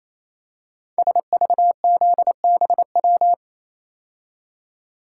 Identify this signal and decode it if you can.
Morse code, heard as a keyed tone, about 29 wpm, 700 Hz. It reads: SVZBW
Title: Code Groups - 5 Letters - 28wpm